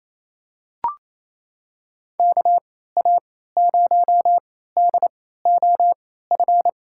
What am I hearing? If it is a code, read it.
Morse code, heard as a keyed tone, about 28 wpm, spaced as for 16 wpm, 700 Hz. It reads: KA0DOF
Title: USA Call Signs - 3x Character Spacing - 28wpm